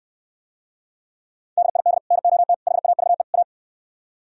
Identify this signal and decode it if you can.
Morse code, heard as a keyed tone, about 67 wpm, 700 Hz. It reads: DID NOT HAVE A